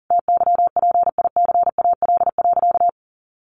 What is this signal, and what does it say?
Morse code, heard as a keyed tone, about 40 wpm, 700 Hz. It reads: TYPICAL.
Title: Sentences from Top 1100 Words 40wpm